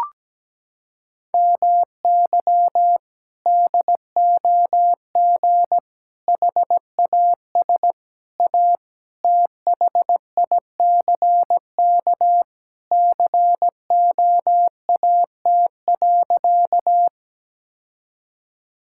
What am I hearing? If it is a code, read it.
Morse code, heard as a keyed tone, about 17 wpm, 700 Hz. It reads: MY DOG HAS A THICK COAT.
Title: Sentences from Top 1600 Words 17wpm